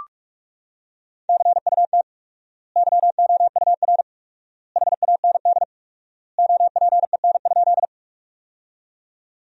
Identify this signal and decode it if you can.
Morse code, heard as a keyed tone, about 46 wpm, 700 Hz. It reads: CUT YOUR HAND OPEN?
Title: 4x Word-Spacing - Sentences from Top 300 Words - 45wpm